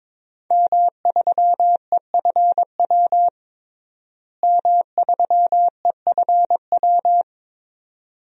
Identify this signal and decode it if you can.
Morse code, heard as a keyed tone, about 22 wpm, 700 Hz. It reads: M3EFW M3EFW
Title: DXCC-100 Call Signs - 5 Characters - Repeated 2x - 22wpm